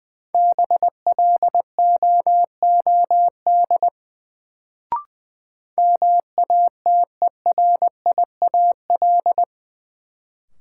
Morse code, heard as a keyed tone, about 20 wpm, 700 Hz. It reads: BLOOD MATERIAL
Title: Sentences from Top 1100 Words - New Word Drill Alt - 20wpm